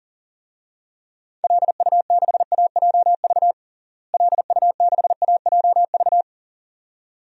Morse code, heard as a keyed tone, about 40 wpm, 700 Hz. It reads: LU6AJV LU6AJV
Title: DXCC-100 Call Signs - 6 Characters - Repeated 2x - 40wpm